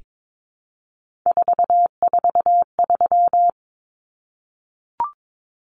Morse code, heard as a keyed tone, about 22 wpm, 700 Hz. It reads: 443
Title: Numbers - 3-Digit 22wpm